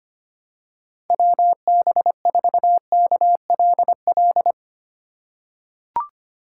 Morse code, heard as a keyed tone, about 25 wpm, 700 Hz. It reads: WB4KLL
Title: USA Call Signs - 6 Characters - 3x Character Spacing - 25wpm